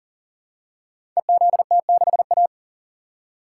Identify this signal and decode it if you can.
Morse code, heard as a keyed tone, about 37 wpm, 700 Hz. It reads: EZT6A
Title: Code Groups - 5 Letter:Numbers - 40wpm